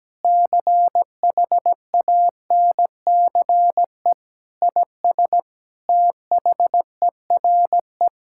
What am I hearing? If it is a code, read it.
Morse code, heard as a keyed tone, about 17 wpm, 700 Hz. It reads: CHANCE IS THERE
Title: Sentences from Top 1000 Words - Review Part 6 - 17wpm